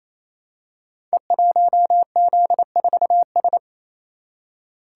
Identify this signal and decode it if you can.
Morse code, heard as a keyed tone, about 28 wpm, 700 Hz. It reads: E1Z4S
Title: Code Groups - 5 Letter:Numbers - 28wpm